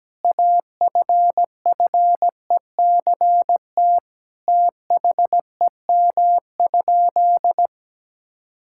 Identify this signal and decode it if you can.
Morse code, heard as a keyed tone, about 17 wpm, 700 Hz. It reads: AFFECT THEM?
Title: Sentences from Top 1000 Words - Review Part 6 - 17wpm